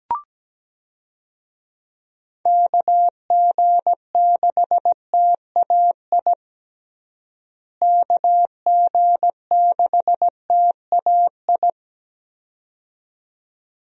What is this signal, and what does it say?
Morse code, heard as a keyed tone, about 17 wpm, 700 Hz. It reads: KG6TAI KG6TAI
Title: USA Call Signs - 6 Characters - Repeated 2x - 17wpm